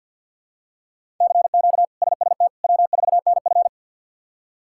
Morse code, heard as a keyed tone, about 50 wpm, 700 Hz. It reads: CQ SST W4NF
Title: K1USN SST QSO Practice - 50wpm